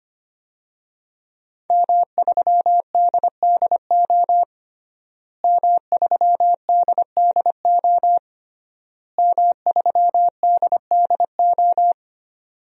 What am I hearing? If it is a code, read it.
Morse code, heard as a keyed tone, about 25 wpm, 700 Hz. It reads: M3DDO M3DDO M3DDO
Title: DXCC-100 Call Signs - 5 Characters - Repeated 3x - 25wpm